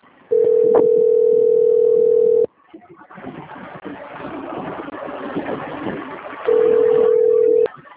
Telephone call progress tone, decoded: ringback tone

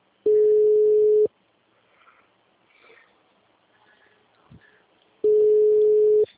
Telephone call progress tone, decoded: ringback tone

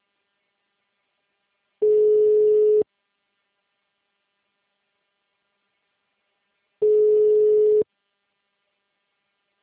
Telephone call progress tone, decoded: ringback tone

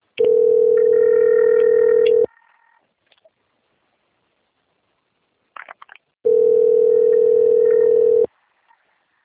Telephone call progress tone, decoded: ringback tone

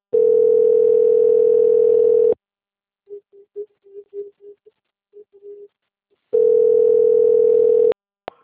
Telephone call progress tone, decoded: ringback tone

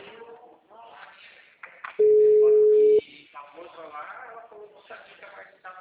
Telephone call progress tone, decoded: ringback tone

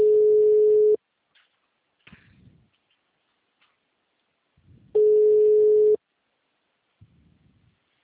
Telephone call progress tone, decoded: ringback tone

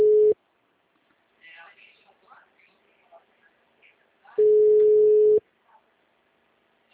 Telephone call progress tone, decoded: ringback tone

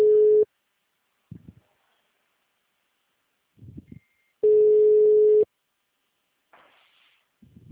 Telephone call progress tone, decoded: ringback tone

